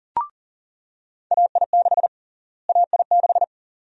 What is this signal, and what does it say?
Morse code, heard as a keyed tone, about 40 wpm, 700 Hz. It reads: AI6 AI6
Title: USA Call Sign Prefixes - Repeated 2x - 40wpm